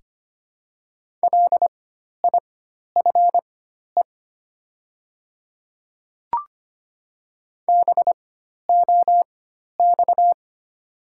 Morse code, heard as a keyed tone, about 25 wpm, spaced as for 12 wpm, 690 Hz. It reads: LIFE BOX
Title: 4x Character Spacing - Top 500 Words - 25wpm